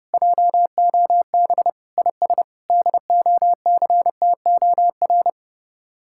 Morse code, heard as a keyed tone, about 35 wpm, 700 Hz. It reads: JOB IS DOCTOR
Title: Speed-Racing Common QSO Phrases Encore - 30wpm